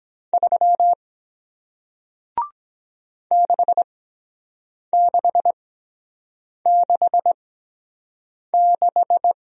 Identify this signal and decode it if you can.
Morse code, heard as a keyed tone, about 23 wpm, spaced as for 7 wpm, 700 Hz. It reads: 3 6666